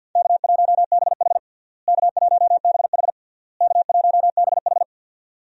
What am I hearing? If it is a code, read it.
Morse code, heard as a keyed tone, about 51 wpm, 700 Hz. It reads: K1BH K1BH K1BH
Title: USA Call Signs - 4 Characters - Repeated 3x - 50wpm